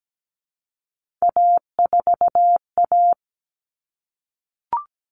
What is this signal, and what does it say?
Morse code, heard as a keyed tone, about 17 wpm, 700 Hz. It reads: A4A
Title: All Cut Numbers - 3-Digit 17wpm